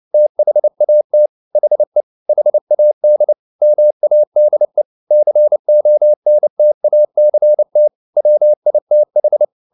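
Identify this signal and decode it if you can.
Morse code, heard as a keyed tone, about 29 wpm, 600 Hz. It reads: THAT HE HAD MADE CONTACT WITH
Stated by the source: Morse code at 29 wpm, sending Ken Follett's novel Never in English, from CW med Gnister.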